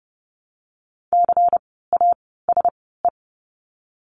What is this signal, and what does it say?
Morse code, heard as a keyed tone, about 30 wpm, spaced as for 17 wpm, 700 Hz. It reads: CASE
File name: Lesson 16f - Letter c - Words 30wpm